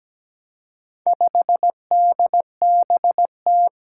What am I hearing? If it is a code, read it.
Morse code, heard as a keyed tone, about 17 wpm, 700 Hz. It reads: 5DBT
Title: Four Challenging Letter-Number Combinations 17wpm